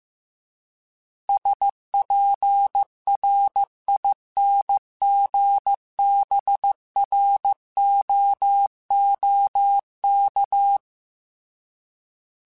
Morse code, heard as a keyed tone, about 15 wpm, 800 Hz. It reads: SPRINGBROOK
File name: fromSMs / FromSMs mp3s / Cities@13